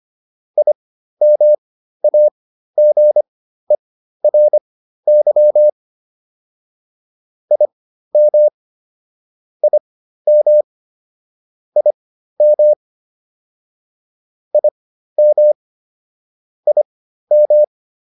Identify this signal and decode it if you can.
Morse code, heard as a keyed tone, about 25 wpm, spaced as for 13 wpm, 600 Hz. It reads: IMAGERY IM IM IM IM IM
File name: IM7 13